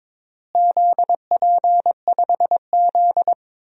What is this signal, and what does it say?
Morse code, heard as a keyed tone, about 22 wpm, 700 Hz. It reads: ZP5Z
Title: DXCC-100 Call Signs - 4 Characters - Repeated 3x - 22wpm